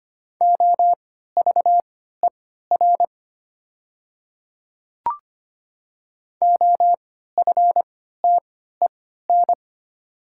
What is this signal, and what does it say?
Morse code, heard as a keyed tone, about 25 wpm, spaced as for 14 wpm, 690 Hz. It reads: OVER OFTEN
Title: 3x Character Spacing - Top 500 Words - 25wpm